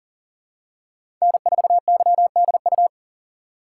Morse code, heard as a keyed tone, about 40 wpm, 700 Hz. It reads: N4YDU
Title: K1USN SST QSO Practice - 40wpm